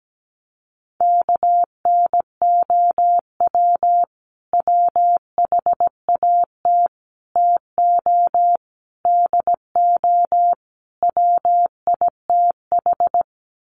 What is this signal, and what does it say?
Morse code, heard as a keyed tone, about 17 wpm, 700 Hz. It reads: KNOW WHAT TO DO WITH
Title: Sets of 5 Words 17wpm